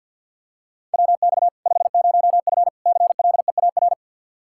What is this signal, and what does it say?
Morse code, heard as a keyed tone, about 51 wpm, 700 Hz. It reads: WX 50F CLEAR
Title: Practice QSO - Exchange 2 - Info - 50wpm